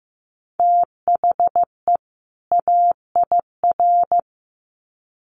Morse code, heard as a keyed tone, about 15 wpm, 700 Hz. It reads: THE AIR
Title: Sets of 2 Words 15wpm